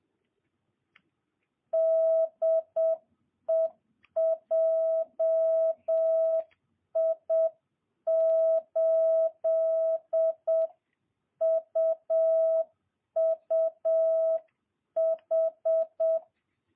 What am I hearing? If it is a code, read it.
Morse code, heard as a keyed tone, about 7 wpm, 650 Hz. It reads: DE JI8UUH
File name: jh8cbh / 2022cw